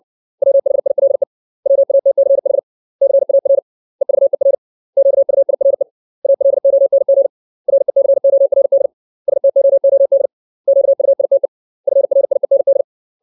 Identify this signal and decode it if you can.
Morse code, heard as a keyed tone, about 60 wpm, 550 Hz. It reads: WHILE WATCH CAR EVER QUIRE ALONG RECORD STOOD QUITE FRIEND
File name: 632 words / 632 Words 60WPM